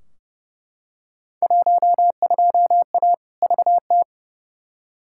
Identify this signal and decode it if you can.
Morse code, heard as a keyed tone, about 29 wpm, 700 Hz. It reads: 12A VT
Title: Field Day Exchanges Encore - 30wpm